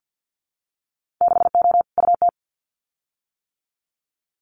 Morse code, heard as a keyed tone, about 50 wpm, 700 Hz. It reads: 6O UT
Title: Winter Field Day Exchanges Encore 50wpm